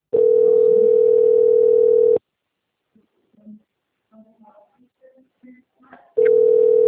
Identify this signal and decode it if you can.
Ringback tone (North America), 440 + 480 Hz, with speech.
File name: out- - - - - .46653